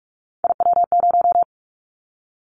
Morse code, heard as a keyed tone, about 44 wpm, 700 Hz. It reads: IW0
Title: Callsign Prefixes for Top 50 DXCC - 45wpm